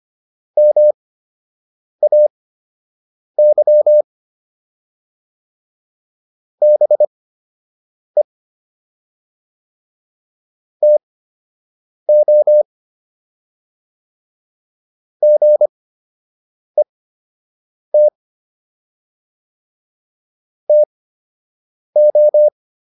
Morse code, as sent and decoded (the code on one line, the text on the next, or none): -- .- -.-- / -... / - --- / --. - / - ---
MAY BE TO GET TO